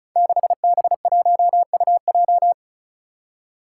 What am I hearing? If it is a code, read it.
Morse code, heard as a keyed tone, about 36 wpm, 700 Hz. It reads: 6B1UJ
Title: Five Challenging Letter-Number Combinations 35wpm